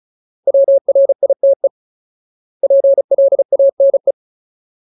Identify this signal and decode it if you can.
Morse code, heard as a keyed tone, about 37 wpm, 550 Hz. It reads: WRITE PLANE
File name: Ultimate Word List / 35WPM Ultimate